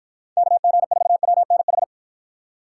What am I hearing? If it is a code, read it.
Morse code, heard as a keyed tone, about 53 wpm, 700 Hz. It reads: KG4WNH